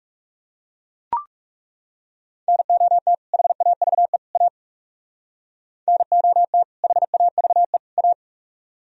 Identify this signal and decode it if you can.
Morse code, heard as a keyed tone, about 38 wpm, 700 Hz. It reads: NOT HAVE A NOT HAVE A